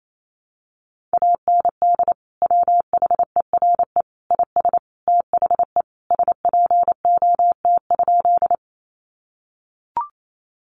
Morse code, heard as a keyed tone, about 28 wpm, 700 Hz. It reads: AND WHERE IS THE SPOT?
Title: Sentences from Top 1100 Words 28wpm